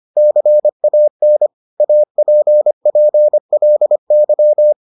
Morse code, heard as a keyed tone, about 25 wpm, 600 Hz. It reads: CAN APPLY